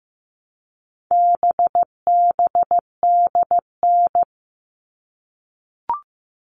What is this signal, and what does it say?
Morse code, heard as a keyed tone, about 15 wpm, 700 Hz. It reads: BBDN
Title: All Cut Numbers - 4-Digit 15wpm